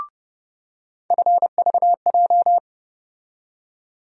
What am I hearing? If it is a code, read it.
Morse code, heard as a keyed tone, about 30 wpm, 700 Hz. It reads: FVJ
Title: USA Call Sign Postfixes - 3 Letters 30wpm